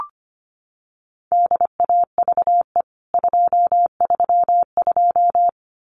Morse code, heard as a keyed tone, about 25 wpm, 700 Hz. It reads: DAVE 232
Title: CWT Exchange 25wpm